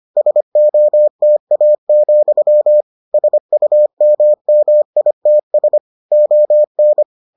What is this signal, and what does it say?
Morse code, heard as a keyed tone, about 25 wpm, 600 Hz. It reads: SOTA, SUMMITS ON